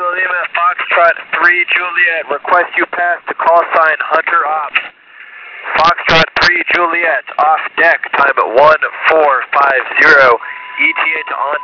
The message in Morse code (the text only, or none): none